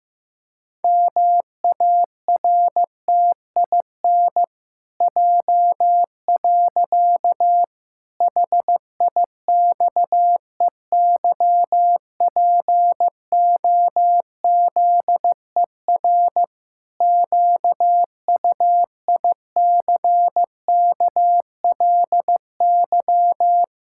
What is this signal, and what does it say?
Morse code, heard as a keyed tone, about 15 wpm, 700 Hz. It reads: MARTIN J. HIXEYPOZER QUICKLY
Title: Pangram Challenge v2 - 15wpm